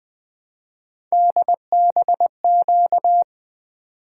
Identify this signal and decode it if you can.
Morse code, heard as a keyed tone, about 20 wpm, 700 Hz. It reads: DBQ
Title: Three Letter-Number Combinations - 20wpm